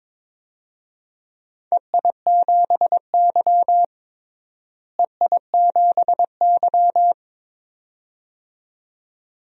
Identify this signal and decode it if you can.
Morse code, heard as a keyed tone, about 22 wpm, 700 Hz. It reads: EI7Y EI7Y
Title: DXCC-100 Call Signs - 4 Characters - Repeated 2x - 22wpm